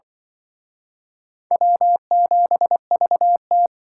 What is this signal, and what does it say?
Morse code, heard as a keyed tone, about 24 wpm, 700 Hz. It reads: W7VT
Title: Speed Racing - USA 1x2 Call Signs 20wpm